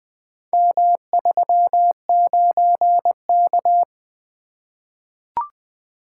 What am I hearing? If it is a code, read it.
Morse code, heard as a keyed tone, about 20 wpm, 700 Hz. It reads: M39K
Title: Four Letter-Number Combinations - 20wpm